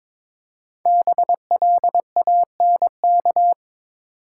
Morse code, heard as a keyed tone, about 23 wpm, 700 Hz. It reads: BLANK
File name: 5 Letter Words - First 3 Letters Given - 22wpm